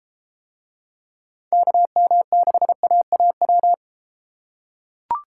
Morse code, heard as a keyed tone, about 32 wpm, 700 Hz. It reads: KM6AAW